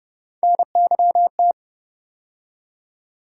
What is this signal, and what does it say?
Morse code, heard as a keyed tone, about 32 wpm, 700 Hz. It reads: NYT
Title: USA Call Sign Postfixes - 3 Letters 30wpm